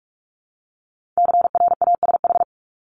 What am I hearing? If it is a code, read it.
Morse code, heard as a keyed tone, about 46 wpm, 690 Hz. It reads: CRASH
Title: Multi-letter Phonemes 45wpm